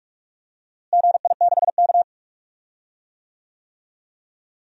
Morse code, heard as a keyed tone, about 41 wpm, 700 Hz. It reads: GI6K
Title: DXCC-100 Call Signs - 4 Characters - Repeated 2x - 45wpm